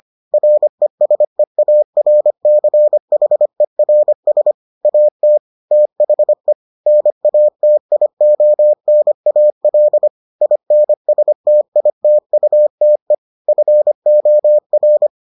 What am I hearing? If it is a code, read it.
Morse code, heard as a keyed tone, about 25 wpm, 600 Hz. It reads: RESEARCHERS AT THE NATIONAL INSTITUTE FOR